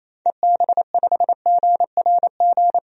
Morse code, heard as a keyed tone, about 28 wpm, 700 Hz. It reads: EB5GRG
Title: DXCC-100 Call Signs - 6 Characters - 28wpm